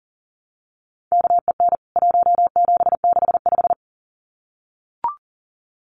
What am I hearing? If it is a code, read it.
Morse code, heard as a keyed tone, about 40 wpm, 700 Hz. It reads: KEN 1765
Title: CWT Exchange 40wpm